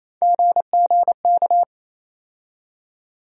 Morse code, heard as a keyed tone, about 28 wpm, 700 Hz. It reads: GGK
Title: USA Call Sign Postfixes - 3 Letters 28wpm